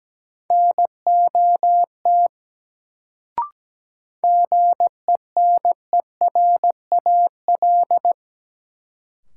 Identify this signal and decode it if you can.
Morse code, heard as a keyed tone, about 17 wpm, 700 Hz. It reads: NOT GENERAL
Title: Sentences from Top 600 Words - New Word Drill - 17wpm